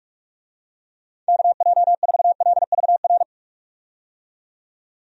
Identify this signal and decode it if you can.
Morse code, heard as a keyed tone, about 47 wpm, 700 Hz. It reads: KJ4LVR
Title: USA Call Signs - 6 Characters - 45wpm